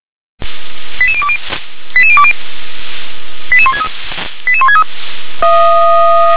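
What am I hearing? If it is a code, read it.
Five-tone selective call (ZVEI), heard as quick runs of five tones, tones of about 70 ms. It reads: 88020 88020 88163 88163
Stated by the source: ZVEI five-tone selective call received by SDR.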